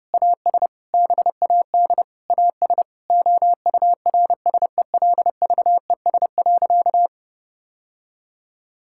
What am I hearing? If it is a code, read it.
Morse code, heard as a keyed tone, about 31 wpm, 700 Hz. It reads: AS BAD AS OURSELVES.
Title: Sentences from Top 1600 Words 30wpm